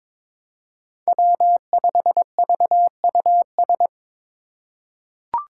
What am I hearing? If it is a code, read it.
Morse code, heard as a keyed tone, about 22 wpm, 700 Hz. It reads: W5VUS
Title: Five Challenging Letter-Number Combinations 22wpm